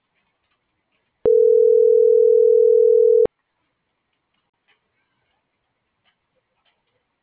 Ringback tone (North America), 440 + 480 Hz.
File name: CENTRAL GENERAL / out yazna.cozumel - .57354